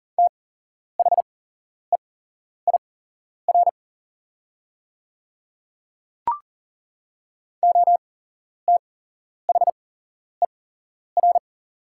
Morse code, heard as a keyed tone, about 40 wpm, spaced as for 11 wpm, 700 Hz. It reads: THEIR OTHER
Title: 8x Character Spacing - Top 100 Words - 40wpm